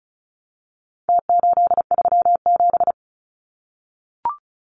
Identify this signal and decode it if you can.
Morse code, heard as a keyed tone, about 34 wpm, 700 Hz. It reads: T837
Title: Numbers - Contest Style - 4-Digit 35wpm